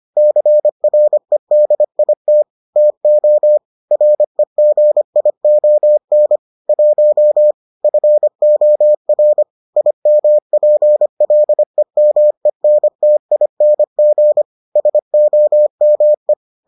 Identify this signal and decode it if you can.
Morse code, heard as a keyed tone, about 25 wpm, 600 Hz. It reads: CREDIT TO REGION 1 FOR IMPLEMENTING SOME